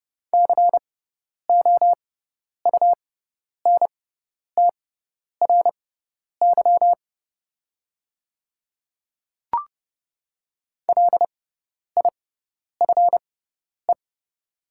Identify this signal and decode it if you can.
Morse code, heard as a keyed tone, about 30 wpm, spaced as for 10 wpm, 690 Hz. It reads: COUNTRY LIFE
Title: 6x Character Spacing - Top 300 Words - 30wpm